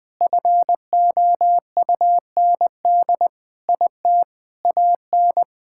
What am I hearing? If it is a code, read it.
Morse code, heard as a keyed tone, about 20 wpm, 700 Hz. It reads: FOUND IT AN
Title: Sentences from Top 1000 Words - Review Part 6 - 20wpm